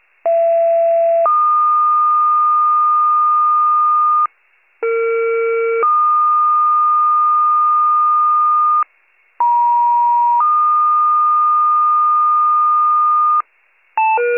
Two-tone page, heard as a one-second tone, then a three-second tone, about 670 then 1220 Hz, about 460 then 1220 Hz, about 950 then 1220 Hz. It is heard from 0.2 to 13.4 s.